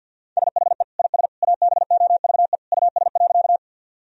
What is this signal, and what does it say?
Morse code, heard as a keyed tone, about 50 wpm, 700 Hz. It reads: SHE IS ABOVE US.
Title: Sentences from Top 1000 Words - Review Part 2 - 50wpm